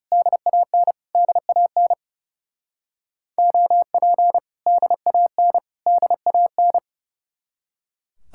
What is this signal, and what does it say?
Morse code, heard as a keyed tone, about 36 wpm, 700 Hz. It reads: DAN DAN OP DAN DAN
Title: Speed-Racing Common QSO Phrases - 30wpm